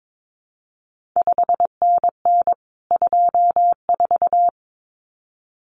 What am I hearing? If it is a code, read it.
Morse code, heard as a keyed tone, about 22 wpm, 700 Hz. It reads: 5NN 24
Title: CQ WW DX Contest Exchange Encore - 22wpm